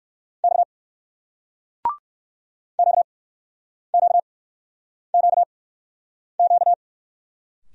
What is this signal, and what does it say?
Morse code, heard as a keyed tone, about 59 wpm, spaced as for 9 wpm, 700 Hz. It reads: X QQQQ